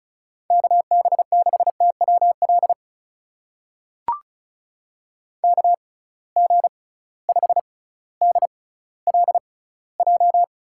Morse code, heard as a keyed tone, about 36 wpm, spaced as for 18 wpm, 700 Hz. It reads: KB6TWL KG5DLJ
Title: USA Call Signs - 6 Characters - 6x Character Spacing - 35wpm